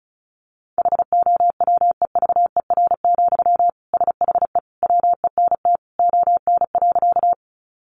Morse code, heard as a keyed tone, about 34 wpm, 700 Hz. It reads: HOWEVER, SHE WENT ON.
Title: Sentences from Top 600 Words 35wpm